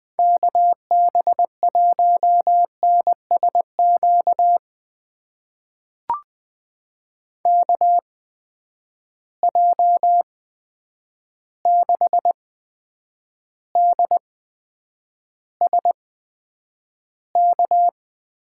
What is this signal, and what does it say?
Morse code, heard as a keyed tone, about 20 wpm, spaced as for 9 wpm, 700 Hz. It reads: KB1NSQ KJ6DSK
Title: USA Call Signs - 6 Characters - 8x Character Spacing - 20wpm